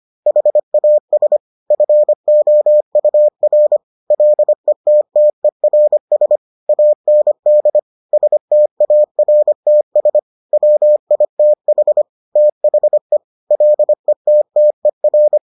Morse code, heard as a keyed tone, about 25 wpm, 600 Hz. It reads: HAS FOUR LETTERS AND STARTS WITH THE LETTER